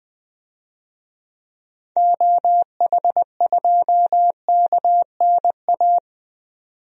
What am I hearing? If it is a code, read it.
Morse code, heard as a keyed tone, about 21 wpm, 700 Hz. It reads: OH2KNA
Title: DXCC-100 Call Signs - 6 Characters - 20wpm